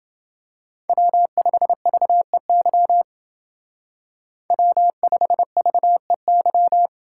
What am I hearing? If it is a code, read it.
Morse code, heard as a keyed tone, about 28 wpm, 700 Hz. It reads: W5VEY W5VEY